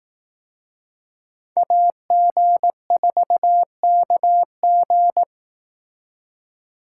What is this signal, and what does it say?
Morse code, heard as a keyed tone, about 18 wpm, 700 Hz. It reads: AG4KG
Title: Speed Racing - USA 2x2 Call Signs 15wpm